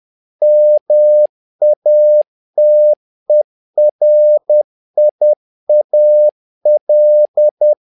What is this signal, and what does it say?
Morse code, heard as a keyed tone, about 10 wpm, 600 Hz. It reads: MATERIAL
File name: learning center / tower trouble / tower trouble 10wpm